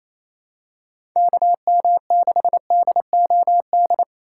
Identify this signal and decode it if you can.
Morse code, heard as a keyed tone, about 27 wpm, 700 Hz. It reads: KM6DOD
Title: USA Call Signs - 6 Characters - 2x Character Spacing - 28wpm